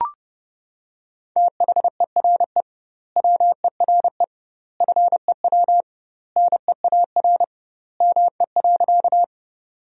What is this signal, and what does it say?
Morse code, heard as a keyed tone, about 30 wpm, 700 Hz. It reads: THERE WERE FEW NEAR ME.
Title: 2x Word-Spacing - Sentences from Top 300 Words - 30wpm